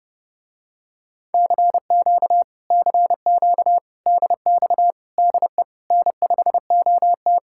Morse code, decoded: CQ CQ DX DE N5OT

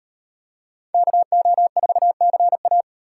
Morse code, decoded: KO4CA

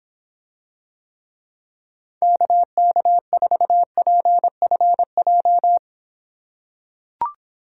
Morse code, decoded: KK4PFJ

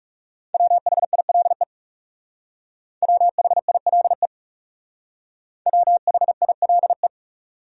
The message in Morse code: WHILE WHILE WHILE

